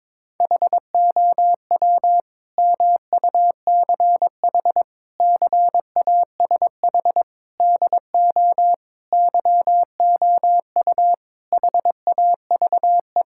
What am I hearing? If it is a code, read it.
Morse code, heard as a keyed tone, about 22 wpm, 700 Hz. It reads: HOW MUCH CASH DO YOU HAVE
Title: Sentences from Top 1000 Words - Review Part 5 - 22wpm